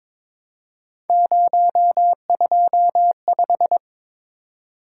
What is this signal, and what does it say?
Morse code, heard as a keyed tone, about 25 wpm, 700 Hz. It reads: 025
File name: Speed-Racing - Three Letter-Number Combinations - 22wpm